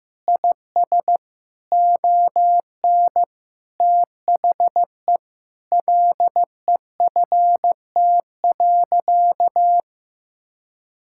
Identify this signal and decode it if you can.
Morse code, heard as a keyed tone, about 15 wpm, 700 Hz. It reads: IS ON THE LEFT.
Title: Sentences from Top 1600 Words 15wpm